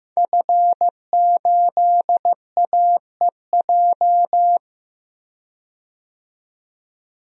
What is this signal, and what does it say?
Morse code, heard as a keyed tone, about 15 wpm, 700 Hz. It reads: F8AEJ
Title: DXCC-100 Call Signs - 5 Characters - 15wpm